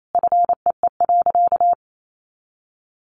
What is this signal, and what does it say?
Morse code, heard as a keyed tone, about 28 wpm, 700 Hz. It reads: FEE.
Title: A Sound Of Thunder 28wpm